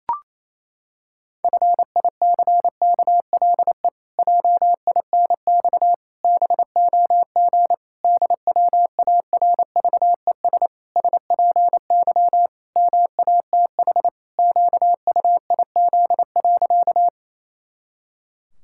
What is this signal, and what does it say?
Morse code, heard as a keyed tone, about 27 wpm, 700 Hz. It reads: FICKLE JINX BOG DWARVES SPY MATH QUIZ.
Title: Pangram Challenge v2 - 28wpm